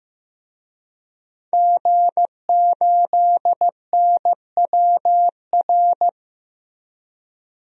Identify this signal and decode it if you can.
Morse code, heard as a keyed tone, about 15 wpm, 700 Hz. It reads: G8NWR
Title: DXCC-100 Call Signs - 5 Characters - 15wpm